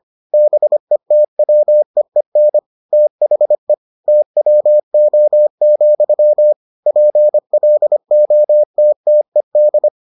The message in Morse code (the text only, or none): BETWEEN THE TWO, PLOTTED